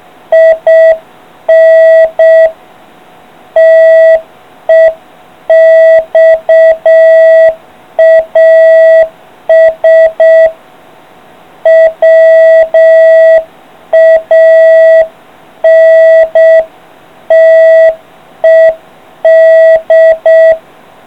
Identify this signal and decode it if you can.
Morse code, heard as a keyed tone, about 6 wpm, 660 Hz. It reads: IN TEXAS WANTED